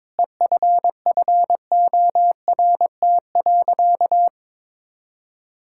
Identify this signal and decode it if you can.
Morse code, heard as a keyed tone, about 22 wpm, 700 Hz. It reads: EFFORT.
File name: Sentences with 7-Word Phrases - 22wpm